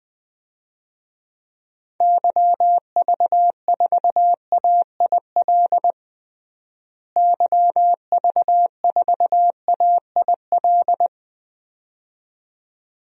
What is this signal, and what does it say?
Morse code, heard as a keyed tone, about 21 wpm, 700 Hz. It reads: YV4AIL YV4AIL
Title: DXCC-100 Call Signs - 6 Characters - Repeated 2x - 20wpm